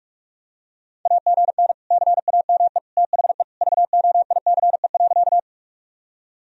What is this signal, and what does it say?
Morse code, heard as a keyed tone, about 46 wpm, 700 Hz. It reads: AGN CAME THE VOICE.
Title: Sentences from Top 500 Words and Abbreviations 45wpm